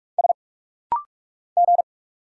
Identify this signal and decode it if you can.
Morse code, heard as a keyed tone, about 46 wpm, 700 Hz. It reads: S G
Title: Single Letters - Rapid-Fire 45wpm